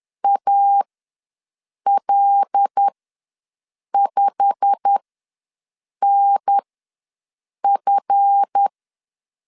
Morse code, heard as a keyed tone, about 10 wpm, spaced as for 6 wpm, 800 Hz. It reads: AL5NF
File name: LEZIONI / 8 YJ45 / 8 YJ45 ws 10wpm 3843